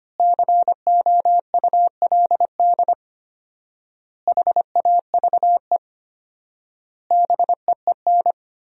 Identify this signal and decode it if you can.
Morse code, heard as a keyed tone, about 25 wpm, 700 Hz. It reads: COULD HAVE BEEN
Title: 4x Word-Spacing - Sentences from Top 500 Words - 25wpm